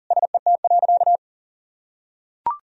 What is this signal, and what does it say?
Morse code, heard as a keyed tone, about 40 wpm, 700 Hz. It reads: SET.
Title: Sentences from Top 1000 Words - Review Part 2 - 40wpm